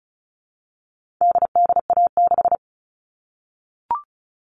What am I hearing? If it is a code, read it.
Morse code, heard as a keyed tone, about 34 wpm, 700 Hz. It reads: DDA6
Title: All Cut Numbers - 4-Digit 35wpm